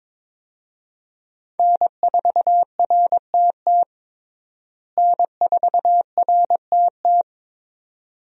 Morse code, heard as a keyed tone, about 22 wpm, 700 Hz. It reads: N4RTT N4RTT
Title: USA Call Signs - 5 Characters - Repeated 2x - 22wpm